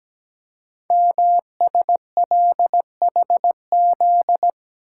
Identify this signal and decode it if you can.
Morse code, heard as a keyed tone, about 17 wpm, 700 Hz. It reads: MSLHZ
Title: Code Groups - 5 Letter:Numbers - 17wpm